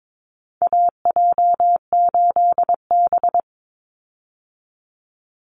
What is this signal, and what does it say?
Morse code, heard as a keyed tone, about 22 wpm, 700 Hz. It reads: AJ8B
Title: Call Signs - 22wpm